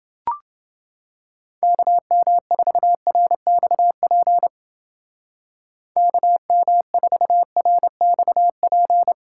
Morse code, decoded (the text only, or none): KM4RXP KM4RXP